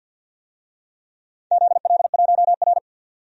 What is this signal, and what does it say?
Morse code, heard as a keyed tone, about 52 wpm, 700 Hz. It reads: ZL1R
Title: DXCC-100 Call Signs - 4 Characters - 50wpm